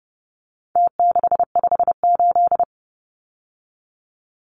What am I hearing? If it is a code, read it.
Morse code, heard as a keyed tone, about 30 wpm, 700 Hz. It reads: T658
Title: Numbers - Contest Style - 4-Digit 30wpm